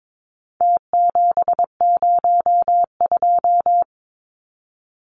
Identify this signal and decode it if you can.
Morse code, heard as a keyed tone, about 22 wpm, 700 Hz. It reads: T702
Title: Numbers - Contest Style - 4-Digit 22wpm